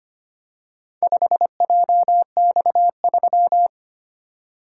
Morse code, heard as a keyed tone, about 25 wpm, 700 Hz. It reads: 5JX3
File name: practice / 25wpm / Four Letter-Number Combinations - 25wpm